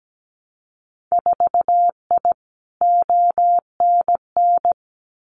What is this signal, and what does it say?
Morse code, heard as a keyed tone, about 17 wpm, 700 Hz. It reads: 4I ONN
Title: Winter Field Day Exchanges Encore 17wpm